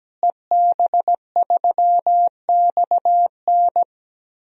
Morse code, heard as a keyed tone, about 17 wpm, 700 Hz. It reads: EB3XN
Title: DXCC-100 Call Signs - 5 Characters - Repeated 3x - 17wpm